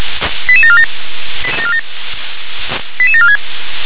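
Five-tone selective call (ZVEI), heard as quick runs of five tones, tones of about 70 ms. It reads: 88747 88746
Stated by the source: ZVEI five-tone selective call 88081 received on an SDR.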